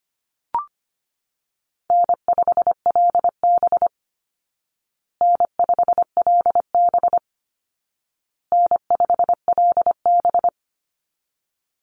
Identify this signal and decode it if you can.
Morse code, heard as a keyed tone, about 25 wpm, 700 Hz. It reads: N5LB N5LB N5LB
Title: Easy Call Signs Repeated 3x - 25wpm